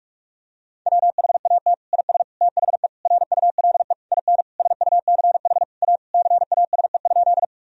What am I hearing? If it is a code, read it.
Morse code, heard as a keyed tone, about 42 wpm, 700 Hz. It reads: WHAT IS THE RULE IN SUCH A CASE?
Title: Sentences from Top 1000 Words - Review Part 2 - 45wpm